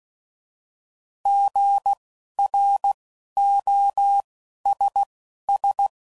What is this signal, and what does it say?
Morse code, heard as a keyed tone, about 16 wpm, spaced as for 11 wpm, 800 Hz. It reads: GROSS
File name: Prefixes / By First Letter